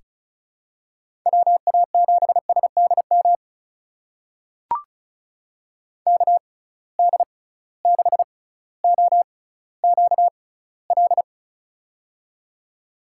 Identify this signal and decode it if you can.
Morse code, heard as a keyed tone, about 36 wpm, spaced as for 18 wpm, 700 Hz. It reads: WA7SDM KD6OQL
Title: USA Call Signs - 6 Characters - 6x Character Spacing - 35wpm